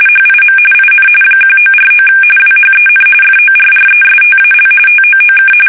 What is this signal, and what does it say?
RTTY, about 45.45 baud. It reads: HIM IN 1949 WHEN FORRESTAL TOOK
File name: fsk850 / usnavy5